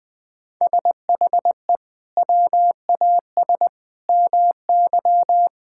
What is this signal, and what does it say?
Morse code, heard as a keyed tone, about 20 wpm, 700 Hz. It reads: SHE WAS MY
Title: Sentences from Top 1000 Words - Review Part 6 - 20wpm